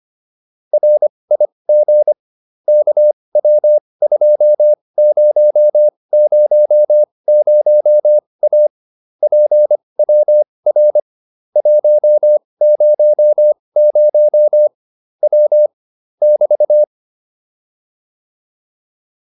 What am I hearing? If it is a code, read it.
Morse code, heard as a keyed tone, about 25 wpm, spaced as for 20 wpm, 600 Hz. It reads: RIG KW2000A PWR 100 W <BT>